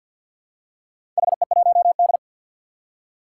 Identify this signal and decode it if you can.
Morse code, heard as a keyed tone, about 49 wpm, 700 Hz. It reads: HE1D
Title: Four Challenging Letter-Number Combinations 50wpm